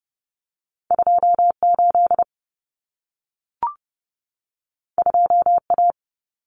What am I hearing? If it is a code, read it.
Morse code, heard as a keyed tone, about 30 wpm, 700 Hz. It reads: 28 2A
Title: Common Cut Numbers - 2-Digit 30wpm